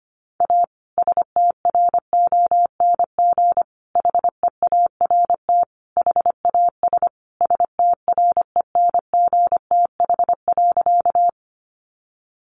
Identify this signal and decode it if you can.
Morse code, heard as a keyed tone, about 25 wpm, 700 Hz. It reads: A STRONG HEART HAS STRENGTH.
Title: Sentences from Top 1300 Words 25wpm